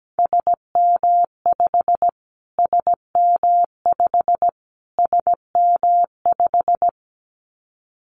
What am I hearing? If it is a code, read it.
Morse code, heard as a keyed tone, about 17 wpm, 700 Hz. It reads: SM5 SM5 SM5
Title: Callsign Prefixes for Top 25 DXCC - Repeated 3x - 17wpm